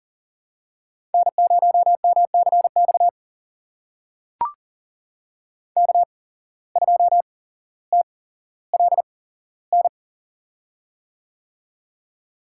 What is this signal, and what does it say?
Morse code, heard as a keyed tone, about 40 wpm, spaced as for 17 wpm, 700 Hz. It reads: N0MCX K2TLN
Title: USA Call Signs - 5 Characters - 8x Character Spacing - 40wpm